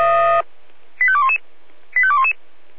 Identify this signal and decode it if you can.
Five-tone selective call (ZVEI), heard as quick runs of five tones, tones of about 70 ms. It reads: 85211 85211
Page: SDR/ZVEI